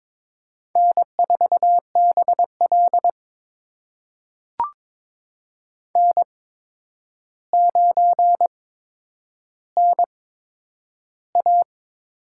Morse code, heard as a keyed tone, about 22 wpm, spaced as for 9 wpm, 700 Hz. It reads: N4BL N9NA
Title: USA Call Signs - 4 Characters - 8x Character Spacing - 22wpm